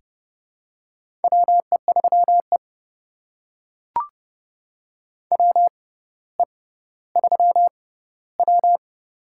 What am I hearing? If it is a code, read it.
Morse code, heard as a keyed tone, about 30 wpm, spaced as for 15 wpm, 700 Hz. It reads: WE3E WE3W
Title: USA Call Signs - 4 Characters - 6x Character Spacing - 30wpm